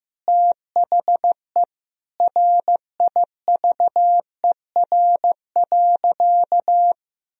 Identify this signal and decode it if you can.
Morse code, heard as a keyed tone, about 15 wpm, 700 Hz. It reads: THE RIVER.
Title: Sentences from Top 1000 Words - Review Part 4 - 15wpm